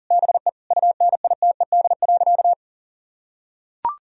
Morse code, heard as a keyed tone, about 40 wpm, 700 Hz. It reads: BE UNITED.
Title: Sentences from Top 1000 Words - Review Part 4 - 40wpm